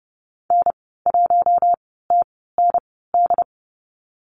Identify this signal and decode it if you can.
Morse code, heard as a keyed tone, about 30 wpm, spaced as for 17 wpm, 700 Hz. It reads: N1TND
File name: Lesson 15g - Number 5 - Call Signs 30wpm